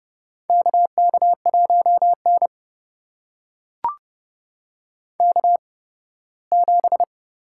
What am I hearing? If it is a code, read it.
Morse code, heard as a keyed tone, about 30 wpm, spaced as for 18 wpm, 700 Hz. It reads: KK1N K7
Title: USA Call Signs - 4 Characters - 8x Character Spacing - 30wpm